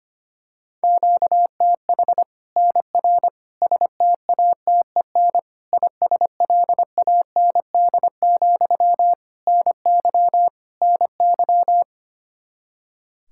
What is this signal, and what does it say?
Morse code, heard as a keyed tone, about 25 wpm, 700 Hz. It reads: QTH NR STATEN ISLAND, NY NY
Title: Practice QSO - Exchange 1 - Beginning - 25wpm